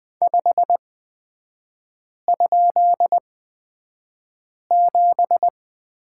Morse code, heard as a keyed tone, about 20 wpm, spaced as for 5 wpm, 700 Hz. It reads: 5?7
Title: 224 Lesson 36e - BT - Three Characters 20wpm